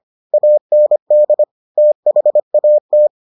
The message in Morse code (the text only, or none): AND THAT